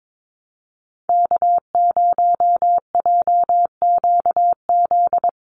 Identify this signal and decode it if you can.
Morse code, heard as a keyed tone, about 22 wpm, 700 Hz. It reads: K0JQZ
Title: NA SOTA Activator Callsigns 22wpm